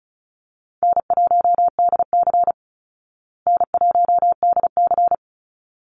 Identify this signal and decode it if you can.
Morse code, heard as a keyed tone, about 34 wpm, 700 Hz. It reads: N1DC N1DC
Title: Easy Call Signs Repeated 2x - 35wpm